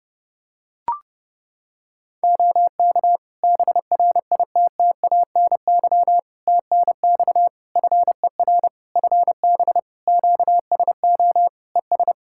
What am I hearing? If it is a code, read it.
Morse code, heard as a keyed tone, about 30 wpm, 700 Hz. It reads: OK BRITTANY TNX FER FB QSO ES
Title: Practice QSO - Exchange 4 - Ending - 30wpm